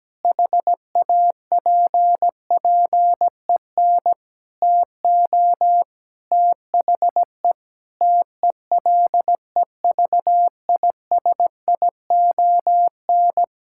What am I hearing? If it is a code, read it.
Morse code, heard as a keyed tone, about 17 wpm, 700 Hz. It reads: HAPPEN TO THE TELEVISION
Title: Sentences from Top 1000 Words - Review Part 5 - 17wpm